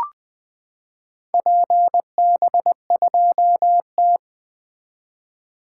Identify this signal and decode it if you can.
Morse code, heard as a keyed tone, about 20 wpm, 700 Hz. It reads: PB2T
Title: 2023 SOTA Activator Call Signs - 20wpm